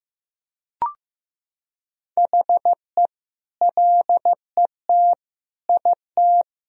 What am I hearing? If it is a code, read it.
Morse code, heard as a keyed tone, about 15 wpm, 700 Hz. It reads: HE LET IT
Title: Sentences from Top 1000 Words - Review Part 1 - 15wpm